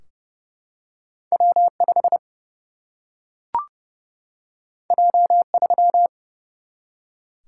Morse code, heard as a keyed tone, about 30 wpm, 700 Hz. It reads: W 5 J 3
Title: Single-Letter Single-Number 30wpm